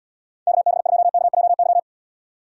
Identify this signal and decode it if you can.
Morse code, heard as a keyed tone, about 71 wpm, 700 Hz. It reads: KB2KJY